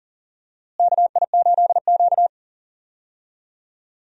Morse code, heard as a keyed tone, about 40 wpm, 700 Hz. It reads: KI8Q
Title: USA Call Signs - 4 Characters - Repeated 2x - 40wpm